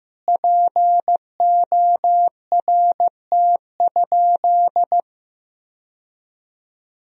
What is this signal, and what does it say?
Morse code, heard as a keyed tone, about 15 wpm, 700 Hz. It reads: PORT?
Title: Sentences from Top 1000 Words - Review Part 1 - 15wpm